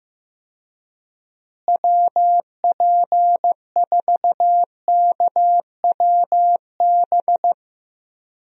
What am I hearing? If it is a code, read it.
Morse code, heard as a keyed tone, about 15 wpm, 700 Hz. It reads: WP4KWB
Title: USA Call Signs - 6 Characters - Repeated 3x - 15wpm